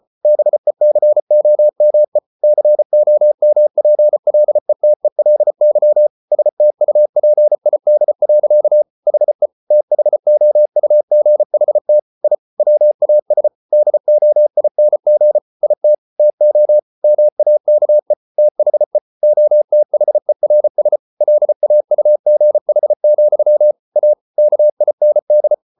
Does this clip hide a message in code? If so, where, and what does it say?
Yes: Morse code from 0.2 to 25.5 s, heard as a keyed tone, about 34 wpm, 600 Hz. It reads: BECOME COMPLETELY STUPID. HE THOUGHT I WAS DOING IT TO MAKE THE OTHERS LAUGH, A KIND